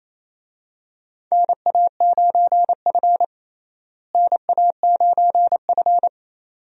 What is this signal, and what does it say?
Morse code, heard as a keyed tone, about 28 wpm, 700 Hz. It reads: NA9F NA9F
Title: USA Call Signs - 4 Characters - Repeated 2x - 28wpm